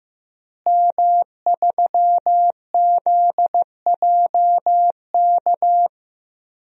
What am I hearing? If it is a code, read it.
Morse code, heard as a keyed tone, about 15 wpm, 700 Hz. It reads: M3ZJK
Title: DXCC-100 Call Signs - 5 Characters - Repeated 2x - 15wpm